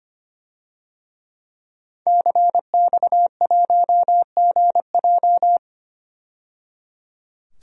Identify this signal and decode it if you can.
Morse code, heard as a keyed tone, about 25 wpm, 700 Hz. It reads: CX1GJ
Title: DXCC-100 Call Signs - 5 Characters - 25wpm